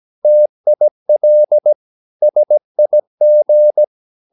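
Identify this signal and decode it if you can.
Morse code, heard as a keyed tone, about 17 wpm, 600 Hz. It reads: TIL SIG
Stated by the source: Slow (QRS) Morse practice at 17 wpm, from CW med Gnister.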